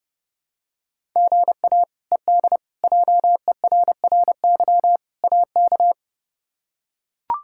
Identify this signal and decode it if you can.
Morse code, heard as a keyed tone, about 31 wpm, 700 Hz. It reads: GA ED JERRY AK
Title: K1USN SST QSO Practice - 30wpm